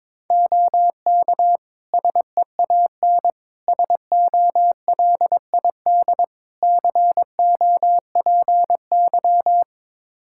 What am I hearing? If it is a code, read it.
Morse code, heard as a keyed tone, about 22 wpm, 700 Hz. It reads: OK SEAN SOLID COPY
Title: Practice QSO - Exchange 3 - More Info - 22wpm